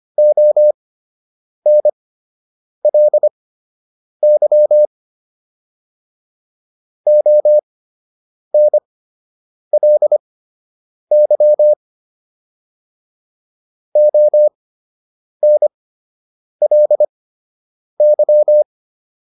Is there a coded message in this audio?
Yes: Morse code from 0.2 to 18.6 s, heard as a keyed tone, about 25 wpm, spaced as for 8 wpm, 600 Hz. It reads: ONLY ONLY ONLY